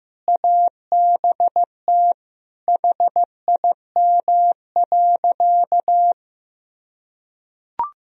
Morse code, heard as a keyed tone, about 15 wpm, 700 Hz. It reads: ABT HIM.
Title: Sentences from Top 200 Words and Abbreviations 15wpm